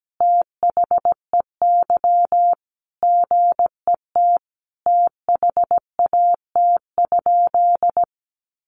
Morse code, heard as a keyed tone, about 17 wpm, 700 Hz. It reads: THEY GET THAT?
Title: Sentences from Top 200 Words 17wpm